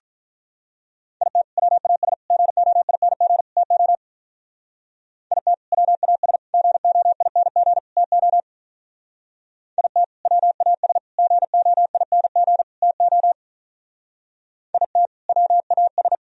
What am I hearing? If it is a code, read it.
Morse code, heard as a keyed tone, about 43 wpm, 700 Hz. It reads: IT WAS GOING TO IT WAS GOING TO IT WAS GOING TO IT WAS
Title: Speed-Racing - Sets of 4 Words 35wpm